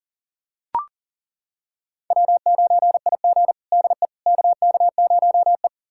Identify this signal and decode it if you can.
Morse code, heard as a keyed tone, about 40 wpm, 700 Hz. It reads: W9IG DE KK0E
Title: USA Callsign DE Callsign - 4 Characters - 40wpm